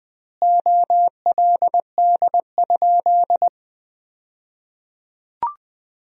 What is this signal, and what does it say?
Morse code, heard as a keyed tone, about 20 wpm, 700 Hz. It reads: OLD?
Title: 2x Word-Spacing - Sentences from Top 400 Words - 20wpm